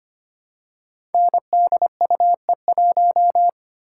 Morse code, heard as a keyed tone, about 25 wpm, 700 Hz. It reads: NDUE1
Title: Code Groups - 5 Letter:Numbers - 25wpm